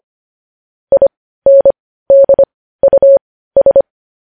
Morse code, heard as a keyed tone, about 25 wpm, spaced as for 15 wpm, 550 Hz. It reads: INDUS
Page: news-headlines-in-morse-code-at-15-wpm